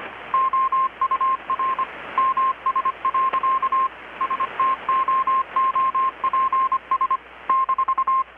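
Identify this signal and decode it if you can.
Morse code, heard as a keyed tone, about 25 wpm, 1060 Hz, with radio noise. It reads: OUR MS. STOOPS -
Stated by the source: Morse from KPH's Night of Nights XX maritime radiotelegraphy event, received in Chiapas, Mexico about 3,800 km away on a Kenwood R-600 with a grounded 10 m long-wire antenna.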